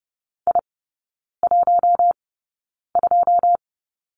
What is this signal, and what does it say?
Morse code, heard as a keyed tone, about 30 wpm, spaced as for 9 wpm, 700 Hz. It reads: I12
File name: 081 Lesson 14e - Number 2 - Three Characters 30wpm